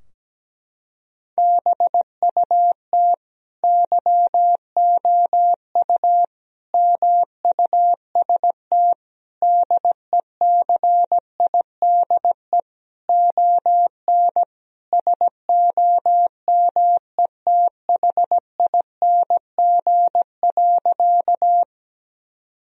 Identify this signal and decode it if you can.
Morse code, heard as a keyed tone, about 17 wpm, 700 Hz. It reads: BUT YOU MUST DECIDE ON SOMETHING.
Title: Sentences from Top 1000 Words - Review Part 3 - 17wpm